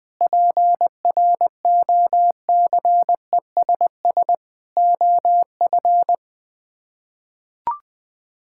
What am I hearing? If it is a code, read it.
Morse code, heard as a keyed tone, about 20 wpm, 700 Hz. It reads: PROCESS OF
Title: Build A Phrase - 4-Word Phrases - 20wpm